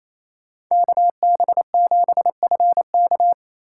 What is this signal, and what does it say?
Morse code, heard as a keyed tone, about 28 wpm, 700 Hz. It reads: KB7FK
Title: USA Call Signs - 5 Characters - 2x Character Spacing - 28wpm